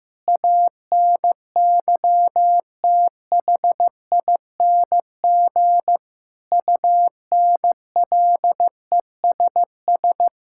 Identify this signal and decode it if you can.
Morse code, heard as a keyed tone, about 16 wpm, 700 Hz. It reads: ANYTHING UNLESS
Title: Sentences from Top 1000 Words - Review Part 6 - 15wpm